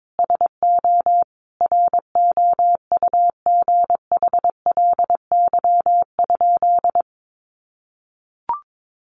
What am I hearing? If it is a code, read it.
Morse code, heard as a keyed tone, about 22 wpm, 700 Hz. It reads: SO ROUGHLY?
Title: Sentences from Top 1500 Words 22wpm